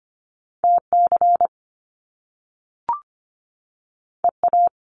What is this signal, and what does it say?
Morse code, heard as a keyed tone, about 25 wpm, 700 Hz. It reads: TC EA
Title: Top Two Letter Combinations in Words 25wpm